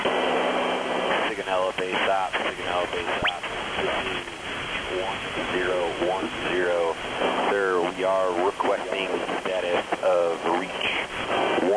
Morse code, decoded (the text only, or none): none